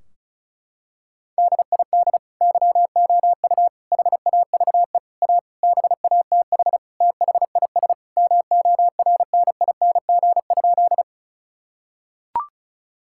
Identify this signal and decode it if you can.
Morse code, heard as a keyed tone, about 36 wpm, 700 Hz. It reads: DID YOU HAVE A BATH THIS MORNING?
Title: Sentences from Top 1600 Words 35wpm